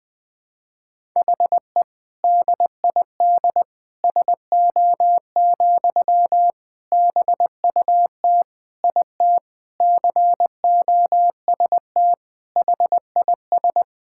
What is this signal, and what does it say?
Morse code, heard as a keyed tone, about 21 wpm, 700 Hz. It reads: HE DID SO, BUT IT COST HIS
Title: Sentences from Top 1000 Words - Review Part 5 - 20wpm